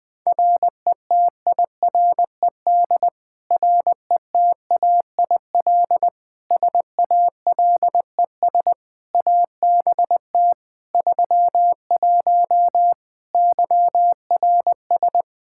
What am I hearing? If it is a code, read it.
Morse code, heard as a keyed tone, about 20 wpm, 700 Hz. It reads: RETIRED RETAIL SALES ABT 31 YRS
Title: Practice QSO - Exchange 3 - More Info - 20wpm